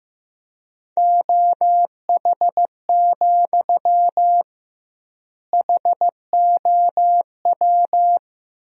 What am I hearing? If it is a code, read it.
Morse code, heard as a keyed tone, about 15 wpm, 700 Hz. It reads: OH, HOW
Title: 2x Word-Spacing - Sentences from Top 500 Words - 15wpm